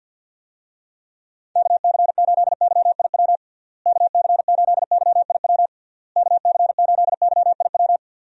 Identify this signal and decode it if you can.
Morse code, heard as a keyed tone, about 46 wpm, 700 Hz. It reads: KC8YIW KC8YIW KC8YIW
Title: USA Call Signs - 6 Characters - Repeated 3x - 50wpm